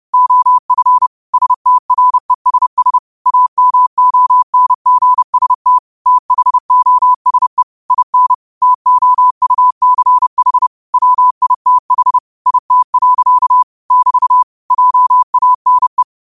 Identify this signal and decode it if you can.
Morse code, heard as a keyed tone, about 30 wpm, 1000 Hz. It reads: OF STRESS AMONGST THOSE IN TOUCH WITH IT. <BT> JANE
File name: qotdfiles / 30wpm